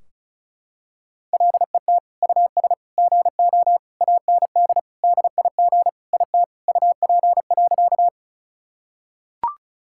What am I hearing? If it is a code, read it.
Morse code, heard as a keyed tone, about 36 wpm, 700 Hz. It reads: LET US GO AND DIG IT UP.
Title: Sentences from Top 1600 Words 35wpm